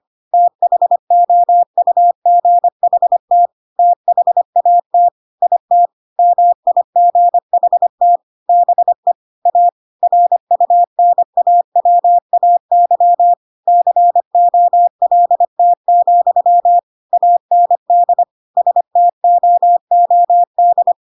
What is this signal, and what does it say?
Morse code, heard as a keyed tone, about 25 wpm, 700 Hz. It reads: THOUGHT THAT IT MIGHT BE A RUNAWAY COLT, AND STOOD
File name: Pinocchio / Pinocchio ch3 25 25 ik1hge